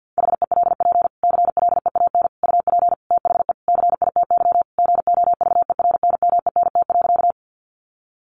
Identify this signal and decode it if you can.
Morse code, heard as a keyed tone, about 48 wpm, 700 Hz. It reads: HELP CLEAN UP THE CITY GOVERNMENT.
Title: Sentences from Top 1100 Words 50wpm